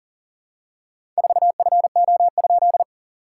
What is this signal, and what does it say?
Morse code, heard as a keyed tone, about 42 wpm, 690 Hz. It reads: 4FO?
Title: SOTA QSO Practice 40wpm